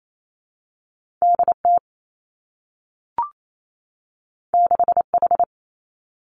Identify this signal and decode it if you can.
Morse code, heard as a keyed tone, about 28 wpm, 700 Hz. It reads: D T 6 H
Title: Two Challenging Letter-Number Combinations 28wpm